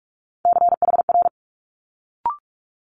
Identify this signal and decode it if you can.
Morse code, heard as a keyed tone, about 46 wpm, 700 Hz. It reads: CHR
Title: Multi-letter Phonemes Quick Fire 45wpm